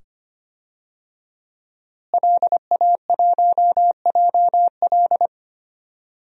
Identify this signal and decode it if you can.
Morse code, heard as a keyed tone, about 25 wpm, 700 Hz. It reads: LA1JL
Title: DXCC-100 Call Signs - 5 Characters - Repeated 2x - 25wpm